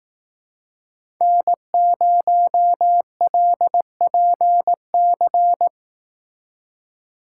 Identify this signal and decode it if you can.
Morse code, heard as a keyed tone, about 19 wpm, 700 Hz. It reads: N0LPC
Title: Speed Racing - USA 1x3 Call Signs 15wpm